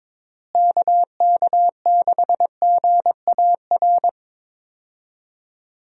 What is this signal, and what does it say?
Morse code, heard as a keyed tone, about 22 wpm, 700 Hz. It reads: KK6GAR
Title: USA Call Signs - 6 Characters - Repeated 2x - 22wpm